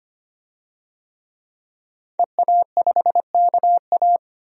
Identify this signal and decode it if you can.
Morse code, heard as a keyed tone, about 25 wpm, 700 Hz. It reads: EA5KA
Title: DXCC-100 Call Signs - 5 Characters - Repeated 3x - 25wpm